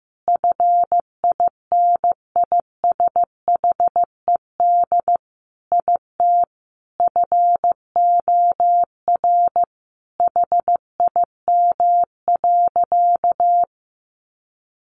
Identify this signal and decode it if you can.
Morse code, heard as a keyed tone, about 15 wpm, 700 Hz. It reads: FINISHED IT FOR HIM.
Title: A Sound Of Thunder 15wpm